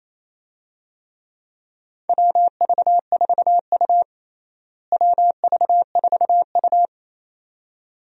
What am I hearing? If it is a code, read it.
Morse code, heard as a keyed tone, about 27 wpm, 700 Hz. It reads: WV4U WV4U
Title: USA Call Signs - 4 Characters - Repeated 2x - 28wpm